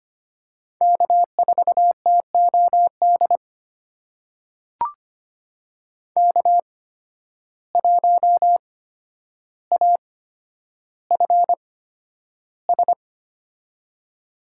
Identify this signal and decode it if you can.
Morse code, heard as a keyed tone, about 25 wpm, spaced as for 11 wpm, 700 Hz. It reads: K4TOD K1AFS
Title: USA Call Signs - 5 Characters - 8x Character Spacing - 25wpm